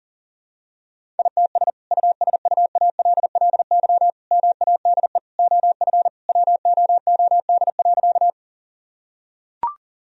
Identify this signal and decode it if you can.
Morse code, heard as a keyed tone, about 40 wpm, 700 Hz. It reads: ITS USUALLY MADE OF WOOD.
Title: Sentences from Top 1000 Words - Review Part 3 - 40wpm